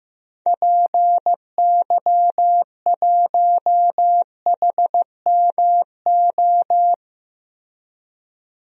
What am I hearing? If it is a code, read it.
Morse code, heard as a keyed tone, about 16 wpm, 700 Hz. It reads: PY1HMO
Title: DXCC-100 Call Signs - 6 Characters - Repeated 2x - 15wpm